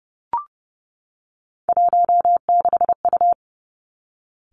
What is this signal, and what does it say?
Morse code, heard as a keyed tone, about 30 wpm, 700 Hz. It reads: 16U
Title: Three Challenging Letter-Number Combinations 30wpm